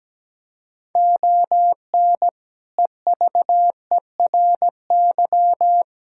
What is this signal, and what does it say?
Morse code, heard as a keyed tone, about 17 wpm, 700 Hz. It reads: ON EVERY
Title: Sentences from Top 1000 Words - Review Part 6 - 17wpm